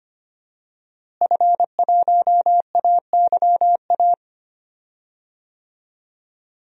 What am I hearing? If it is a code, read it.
Morse code, heard as a keyed tone, about 25 wpm, 700 Hz. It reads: F1AYA
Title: DXCC-100 Call Signs - 5 Characters - Repeated 2x - 25wpm